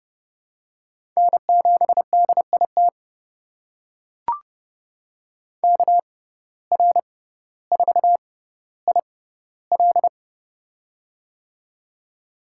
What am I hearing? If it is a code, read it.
Morse code, heard as a keyed tone, about 30 wpm, spaced as for 15 wpm, 700 Hz. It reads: N7DIT KR4IL
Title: USA Call Signs - 5 Characters - 6x Character Spacing - 30wpm